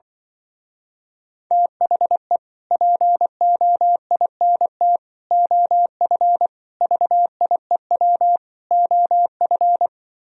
Morse code, heard as a keyed tone, about 25 wpm, 700 Hz. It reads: THE POINT OF VIEW OF